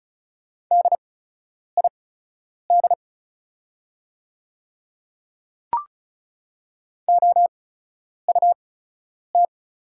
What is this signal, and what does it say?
Morse code, heard as a keyed tone, about 36 wpm, spaced as for 10 wpm, 690 Hz. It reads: DID OUT
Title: 8x Character Spacing - Top 200 Words - 35wpm